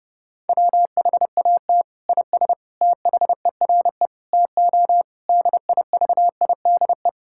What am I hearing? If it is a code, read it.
Morse code, heard as a keyed tone, about 30 wpm, 700 Hz. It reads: WHAT IS THERE TO DIVIDE
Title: Sentences from Top 1600 Words 30wpm